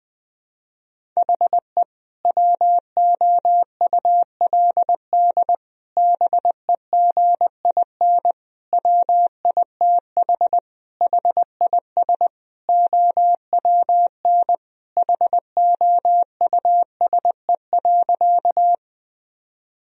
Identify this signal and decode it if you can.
Morse code, heard as a keyed tone, about 20 wpm, 700 Hz. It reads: HE WOULD BEGIN WITH HIS OWN HOUSE.
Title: Sentences from Top 1000 Words - Review Part 1 - 20wpm